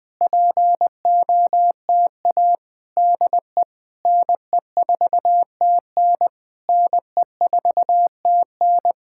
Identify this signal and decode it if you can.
Morse code, heard as a keyed tone, about 20 wpm, 700 Hz. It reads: POTA DE NE4TN NE4TN
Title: Speed-Racing Common QSO Phrases - 17wpm